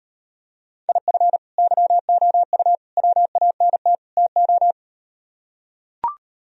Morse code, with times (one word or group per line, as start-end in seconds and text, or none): 0.9-1.4 IF
1.6-2.8 YOU
3.0-4.0 WANT
4.2-4.7 TO